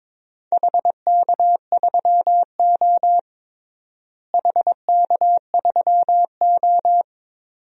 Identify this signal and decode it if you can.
Morse code, heard as a keyed tone, about 22 wpm, 700 Hz. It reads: HK3O HK3O
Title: DXCC-100 Call Signs - 4 Characters - Repeated 2x - 22wpm